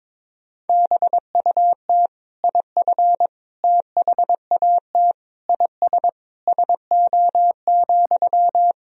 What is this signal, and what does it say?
Morse code, heard as a keyed tone, about 22 wpm, 700 Hz. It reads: BUT IF THAT IS SO,
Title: Sentences from Top 200 Words and Abbreviations 22wpm